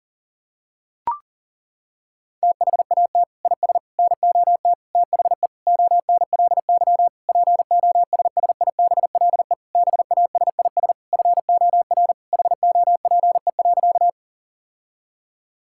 Morse code, heard as a keyed tone, about 40 wpm, 700 Hz. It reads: THAT IS NOT THE ONLY POSSIBLE BASIS FOR HOPE.
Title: Sentences from Top 1000 Words - Review Part 5 - 40wpm